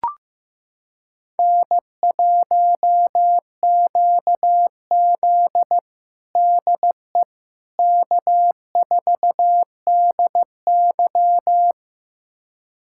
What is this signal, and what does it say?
Morse code, heard as a keyed tone, about 15 wpm, 700 Hz. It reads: N1QZ DE K4DY
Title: USA Callsign DE Callsign - 4 Characters - 15wpm